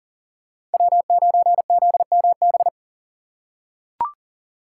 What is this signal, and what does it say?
Morse code, heard as a keyed tone, about 40 wpm, 700 Hz. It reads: W9ZMB
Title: USA Call Signs - 5 Characters - 4x Character Spacing - 40wpm